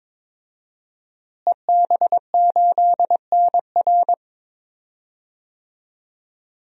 Morse code, heard as a keyed tone, about 22 wpm, 700 Hz. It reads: EB8NR
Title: DXCC-100 Call Signs - 5 Characters - Repeated 2x - 22wpm